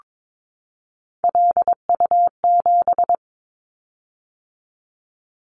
Morse code, heard as a keyed tone, about 22 wpm, 700 Hz. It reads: LU7
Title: Callsign Prefixes for Top 50 DXCC - 22wpm